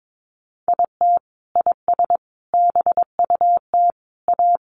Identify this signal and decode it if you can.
Morse code, heard as a keyed tone, about 22 wpm, 700 Hz. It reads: IT IS BUT A
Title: Sentences from Top 1000 Words 22wpm